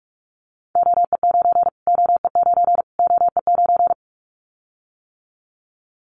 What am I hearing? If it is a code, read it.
Morse code, heard as a keyed tone, about 44 wpm, 700 Hz. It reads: OE9 OE9 OE9
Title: Callsign Prefixes for Top 25 DXCC - Repeated 3x - 45wpm